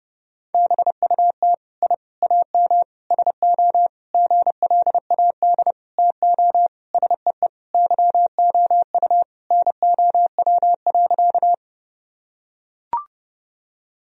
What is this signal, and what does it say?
Morse code, heard as a keyed tone, about 32 wpm, 700 Hz. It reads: BUT I AM SO GLAD TO SEE YOU NOW.
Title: Sentences from Top 1000 Words - Review Part 2 - 30wpm